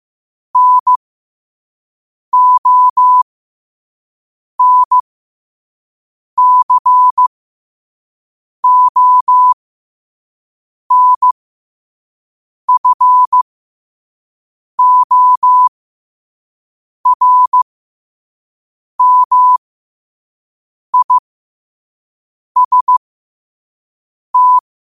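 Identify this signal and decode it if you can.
Morse code, heard as a keyed tone, about 15 wpm, spaced as for 5 wpm, 1000 Hz. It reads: NONCONFORMIST
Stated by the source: Morse code sent at 5 wpm.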